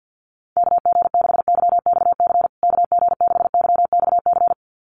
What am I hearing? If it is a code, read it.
Morse code, heard as a keyed tone, about 50 wpm, 700 Hz. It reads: KG6YXC KG6YXC
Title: USA Call Signs Repeated 2x 50wpm